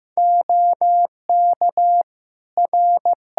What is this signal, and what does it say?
Morse code, heard as a keyed tone, about 16 wpm, 700 Hz. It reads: OK R
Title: Practice QSO - Exchange 4 - Ending - 15wpm